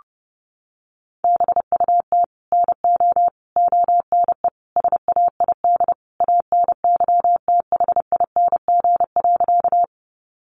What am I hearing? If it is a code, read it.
Morse code, heard as a keyed tone, about 30 wpm, 700 Hz. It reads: BUT NO ONE SAID ANYTHING.
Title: Sentences from Top 1000 Words 30wpm